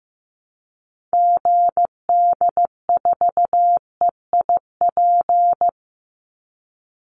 Morse code, heard as a keyed tone, about 15 wpm, 700 Hz. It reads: GD4EIP
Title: Call Signs Repeated 3x - 15wpm